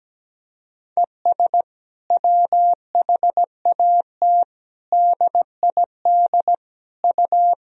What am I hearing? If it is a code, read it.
Morse code, heard as a keyed tone, about 17 wpm, 700 Hz. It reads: ES WHAT DID U
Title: Sentences from Top 100 Words and Abbreviations 17wpm